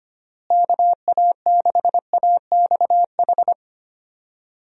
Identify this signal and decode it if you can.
Morse code, heard as a keyed tone, about 25 wpm, 700 Hz. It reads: KA6AXH
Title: USA Call Signs - 6 Characters - Repeated 3x - 25wpm